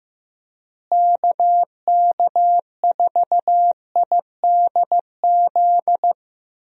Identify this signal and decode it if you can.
Morse code, heard as a keyed tone, about 15 wpm, 700 Hz. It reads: KK4IDZ
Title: USA Call Signs - 6 Characters - Repeated 2x - 15wpm